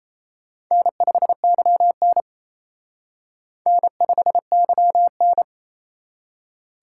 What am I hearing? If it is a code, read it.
Morse code, heard as a keyed tone, about 35 wpm, 700 Hz. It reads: N5YN N5YN
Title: Speed Racing - USA 1x2 Call Signs 28wpm